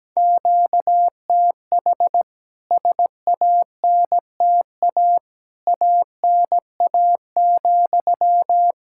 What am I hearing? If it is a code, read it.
Morse code, heard as a keyed tone, about 17 wpm, 700 Hz. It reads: QTH SANTA ANA,
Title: K1USN SST QSO Practice 17wpm